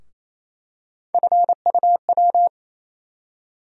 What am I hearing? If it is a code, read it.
Morse code, heard as a keyed tone, about 28 wpm, 700 Hz. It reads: FUW
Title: USA Call Sign Postfixes - 3 Letters 28wpm